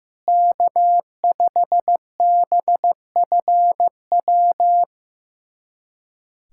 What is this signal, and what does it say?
Morse code, heard as a keyed tone, about 16 wpm, 700 Hz. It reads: K5BFW
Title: USA Callsign DE Callsign 15wpm